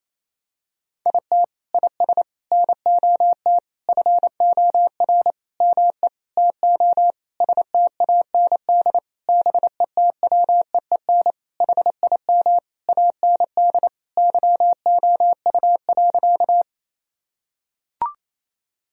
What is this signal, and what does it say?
Morse code, heard as a keyed tone, about 28 wpm, 700 Hz. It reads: IT IS NOT FOR ME TO STAND BETWEEN HIM AND YOU.
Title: Sentences from Top 1000 Words - Review Part 1 - 28wpm